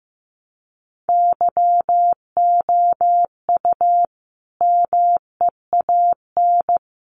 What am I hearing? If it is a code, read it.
Morse code, heard as a keyed tone, about 15 wpm, 700 Hz. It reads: YOU MEAN
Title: Sentences from Top 600 Words 15wpm